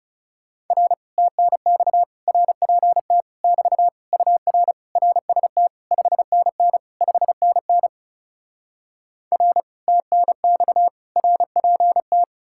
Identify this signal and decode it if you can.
Morse code, heard as a keyed tone, about 40 wpm, 700 Hz. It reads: R TNX RPT <BT> UR RST 5NN 5NN R TNX RPT